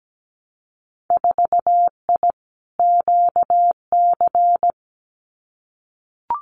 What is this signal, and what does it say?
Morse code, heard as a keyed tone, about 17 wpm, 700 Hz. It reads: 4I QC
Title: Winter Field Day Exchanges Encore 17wpm